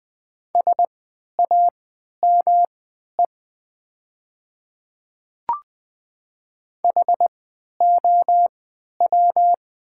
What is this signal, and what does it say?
Morse code, heard as a keyed tone, about 20 wpm, spaced as for 11 wpm, 700 Hz. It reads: SAME HOW
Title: 3x Character Spacing - Top 500 Words - 20wpm